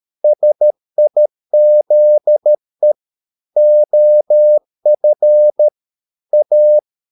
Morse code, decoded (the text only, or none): SIZE OF A